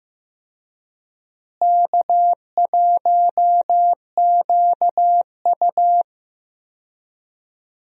Morse code, heard as a keyed tone, about 15 wpm, 700 Hz. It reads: K1QU
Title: USA Call Signs - 4 Characters - Repeated 3x - 15wpm